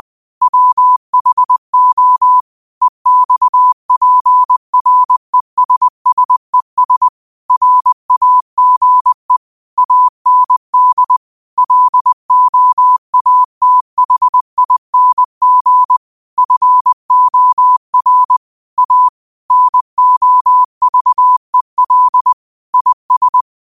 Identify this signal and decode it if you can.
Morse code, heard as a keyed tone, about 20 wpm, 1000 Hz. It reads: WHO EXPRESSES RAGE AND LOATHING FOR A NOVEL IS